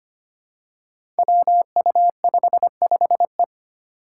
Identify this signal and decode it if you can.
Morse code, heard as a keyed tone, about 25 wpm, 700 Hz. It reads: WU55E
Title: Five Challenging Letter-Number Combinations 25wpm